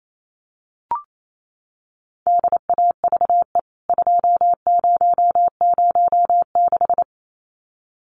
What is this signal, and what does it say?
Morse code, heard as a keyed tone, about 28 wpm, 700 Hz. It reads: DAVE 2006
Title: CWT Exchange 28wpm